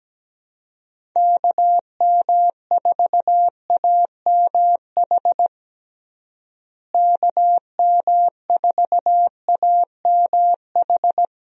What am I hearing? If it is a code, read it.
Morse code, heard as a keyed tone, about 17 wpm, 700 Hz. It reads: KM4AMH KM4AMH
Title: USA Call Signs - 6 Characters - Repeated 3x - 17wpm